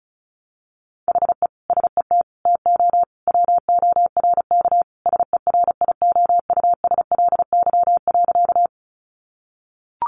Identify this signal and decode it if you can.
Morse code, heard as a keyed tone, about 34 wpm, 700 Hz. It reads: HE SET TO WORK SERIOUSLY.
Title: Sentences from Top 1200 Words 35wpm